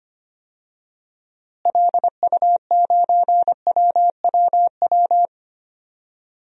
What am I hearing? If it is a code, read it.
Morse code, heard as a keyed tone, about 25 wpm, 700 Hz. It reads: LU9WWW
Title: DXCC-100 Call Signs - 6 Characters - Repeated 2x - 25wpm